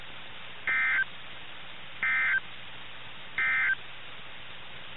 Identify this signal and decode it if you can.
Emergency Alert System (SAME) tones, heard as harsh buzzing bursts. It is heard from 0.6 to 3.8 s.
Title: NOAA Weather Radio - KIH61